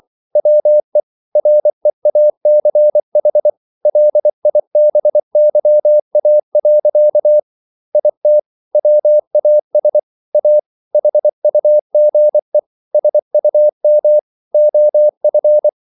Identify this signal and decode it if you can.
Morse code, heard as a keyed tone, about 24 wpm, 600 Hz. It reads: WE REACH LIBYA. IT WAS A HUGE SUM OF